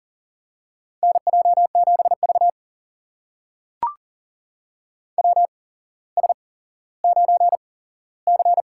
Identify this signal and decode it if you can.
Morse code, heard as a keyed tone, about 40 wpm, spaced as for 17 wpm, 700 Hz. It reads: NJ7V WS9C
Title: USA Call Signs - 4 Characters - 8x Character Spacing - 40wpm